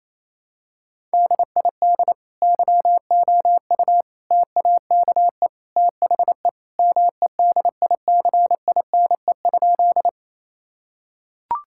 Morse code, heard as a keyed tone, about 28 wpm, 700 Hz. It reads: DID YOU TAKE THE MEDICINE?
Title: Sentences from Top 1000 Words - Review Part 6 - 28wpm